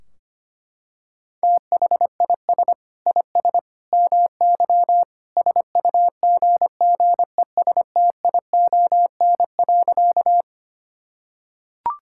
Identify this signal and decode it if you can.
Morse code, heard as a keyed tone, about 25 wpm, 700 Hz. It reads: THIS IS MY SUGGESTION.
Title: Sentences from Top 1600 Words 25wpm